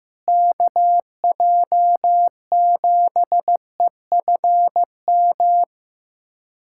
Morse code, heard as a keyed tone, about 15 wpm, 700 Hz. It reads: KJ7EFM
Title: USA Callsign DE Callsign - 6 Characters - 15wpm